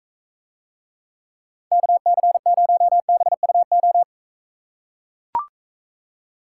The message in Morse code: KC0BUO

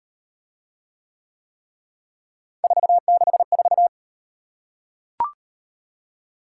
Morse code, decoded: 464

